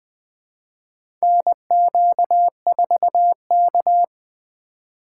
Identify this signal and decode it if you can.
Morse code, heard as a keyed tone, about 20 wpm, 700 Hz. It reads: NQ4K
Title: USA Call Signs - 4 Characters - 4x Character Spacing - 20wpm